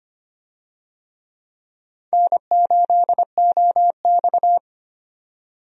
Morse code, heard as a keyed tone, about 25 wpm, 700 Hz. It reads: N8OX
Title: USA Call Signs - 4 Characters - 25wpm